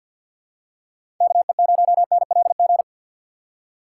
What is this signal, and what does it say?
Morse code, heard as a keyed tone, about 47 wpm, 700 Hz. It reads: KE0NLG
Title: USA Call Signs - 4x Character Spacing - 50wpm